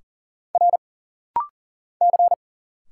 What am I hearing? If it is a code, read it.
Morse code, heard as a keyed tone, about 39 wpm, 700 Hz. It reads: R C
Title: Single Letter-Number - Rapid-Fire 40wpm